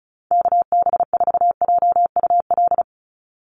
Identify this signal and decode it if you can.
Morse code, heard as a keyed tone, about 34 wpm, 700 Hz. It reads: KB4JUL
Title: USA Call Signs 35wpm